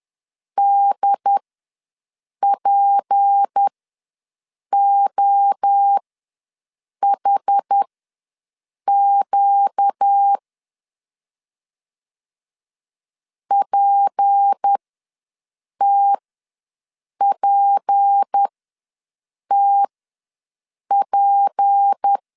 Morse code, heard as a keyed tone, about 10 wpm, spaced as for 6 wpm, 800 Hz. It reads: DPOHQ PTPTP
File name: LEZIONI / 6 PKX0 / 6 P ws 10wpm 3751